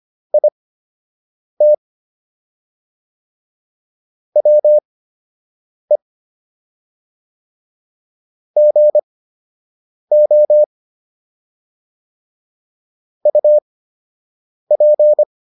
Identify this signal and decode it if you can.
Morse code, heard as a keyed tone, about 25 wpm, spaced as for 7 wpm, 600 Hz. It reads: IT WE GO UP